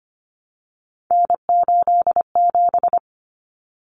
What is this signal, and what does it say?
Morse code, heard as a keyed tone, about 25 wpm, 700 Hz. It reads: N87
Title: Common Cut Numbers - 3-Digit 25wpm